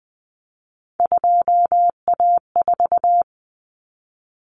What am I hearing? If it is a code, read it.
Morse code, heard as a keyed tone, about 20 wpm, 700 Hz. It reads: 2A4
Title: Common Cut Numbers - 3-Digit 20wpm